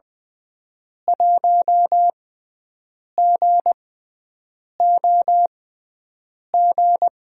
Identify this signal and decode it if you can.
Morse code, heard as a keyed tone, about 21 wpm, spaced as for 7 wpm, 700 Hz. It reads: 1GOG
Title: USA Call Signs - 5 Characters - 6x Character Spacing - 20wpm